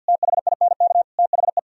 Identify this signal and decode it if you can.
Morse code, heard as a keyed tone, about 64 wpm, 700 Hz. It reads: THINK THE